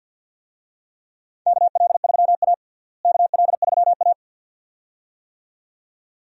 Morse code, heard as a keyed tone, about 50 wpm, 700 Hz. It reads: KL3A KL3A
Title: USA Call Signs - 4 Characters - Repeated 2x - 50wpm